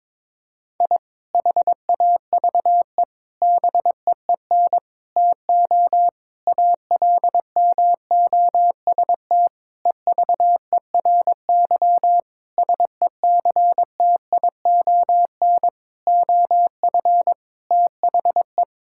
Morse code, heard as a keyed tone, about 22 wpm, 700 Hz. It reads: I HAVE BEEN TO ALMOST EVERY SECTION OF THE
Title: Sentences from Top 1000 Words - Review Part 5 - 22wpm